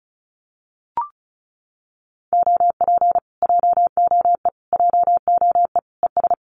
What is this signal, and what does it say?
Morse code, heard as a keyed tone, about 34 wpm, 700 Hz. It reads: OP JOE JOE ES
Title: Common QSO Phrases Encore 35wpm